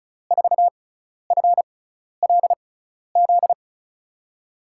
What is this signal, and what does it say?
Morse code, heard as a keyed tone, about 34 wpm, spaced as for 12 wpm, 700 Hz. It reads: 4FLZ
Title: USA Call Signs - 6 Characters - 6x Character Spacing - 35wpm